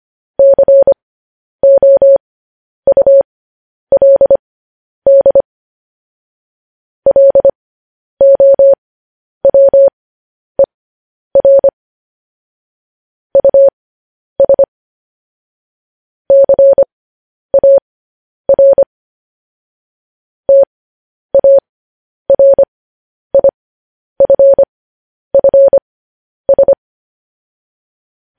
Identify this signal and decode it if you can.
Morse code, heard as a keyed tone, about 25 wpm, spaced as for 10 wpm, 550 Hz. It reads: COULD LOWER US CAR TARIFFS
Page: news headlines in morse code at 10 wpm